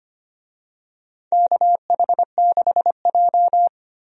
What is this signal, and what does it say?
Morse code, heard as a keyed tone, about 25 wpm, 700 Hz. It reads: KH6J
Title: DXCC-100 Call Signs - 4 Characters - 25wpm